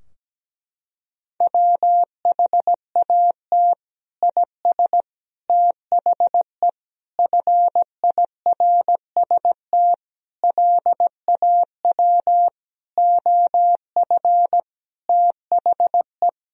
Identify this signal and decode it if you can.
Morse code, heard as a keyed tone, about 17 wpm, 700 Hz. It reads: WHAT IS THE FIRST LAW OF THE
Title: Sentences from Top 1000 Words - Review Part 3 - 17wpm